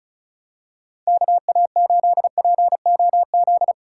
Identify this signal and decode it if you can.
Morse code, heard as a keyed tone, about 34 wpm, 700 Hz. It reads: KA8POZ
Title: K1USN SST QSO Practice - 35wpm